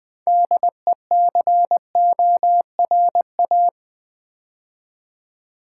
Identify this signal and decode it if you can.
Morse code, heard as a keyed tone, about 20 wpm, 700 Hz. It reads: DECORA
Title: Build A Word - 2 Letters At A Time - 8-Letter Words - 20wpm